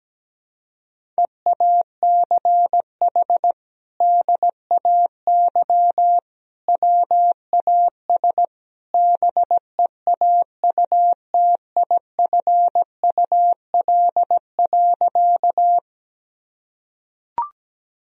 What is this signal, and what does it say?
Morse code, heard as a keyed tone, about 17 wpm, 700 Hz. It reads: EACH DAY WAS BEAUTIFUL.
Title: Sentences from Top 1000 Words - Review Part 1 - 17wpm